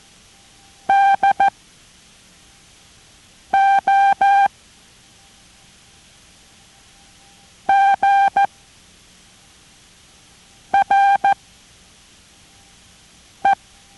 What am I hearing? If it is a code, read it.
Morse code, heard as a keyed tone, about 14 wpm, 790 Hz. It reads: D O G R E